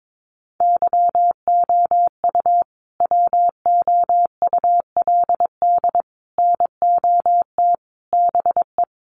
Morse code, heard as a keyed tone, about 22 wpm, 700 Hz. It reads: YOU WOULD NOT BE
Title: Sets of 5 Words Encore2 22wpm